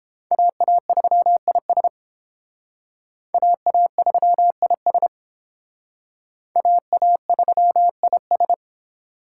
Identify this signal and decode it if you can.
Morse code, heard as a keyed tone, about 31 wpm, 700 Hz. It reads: AA3IS AA3IS AA3IS